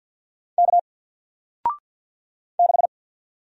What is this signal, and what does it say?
Morse code, heard as a keyed tone, about 53 wpm, 700 Hz. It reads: K 6
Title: Single Letter-Number 50wpm